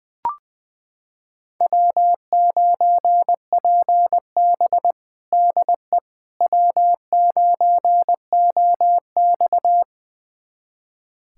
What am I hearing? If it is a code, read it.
Morse code, heard as a keyed tone, about 21 wpm, 700 Hz. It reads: W9PB DE W9OX
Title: USA Callsign DE Callsign - 4 Characters - 20wpm